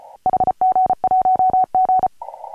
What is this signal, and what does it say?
Morse code, heard as a keyed tone, about 34 wpm, 740 Hz. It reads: HG1G
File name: AUDIO2019 / HG1G CW